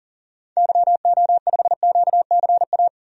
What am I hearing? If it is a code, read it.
Morse code, heard as a keyed tone, about 40 wpm, 700 Hz. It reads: YO5QCA
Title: DXCC-100 Call Signs - 6 Characters - 40wpm